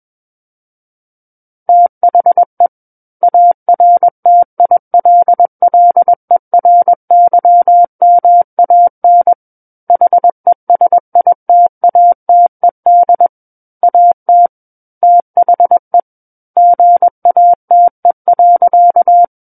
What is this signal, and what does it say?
Morse code, heard as a keyed tone, about 21 wpm, 700 Hz. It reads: THE ARTILLERYMAN HESITATED AT THE GATE.
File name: Book One / 12-Chapter 12 - 21 WPM